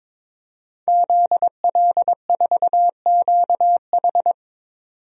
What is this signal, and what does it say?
Morse code, heard as a keyed tone, about 22 wpm, 700 Hz. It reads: ZL4QH
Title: DXCC-100 Call Signs - 5 Characters - 22wpm